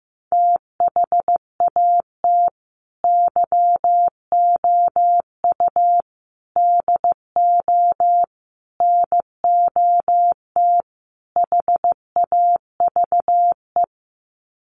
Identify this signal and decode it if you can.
Morse code, heard as a keyed tone, about 15 wpm, 700 Hz. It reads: THAT YOU DO NOT HAVE
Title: Sets of 5 Words Encore 15wpm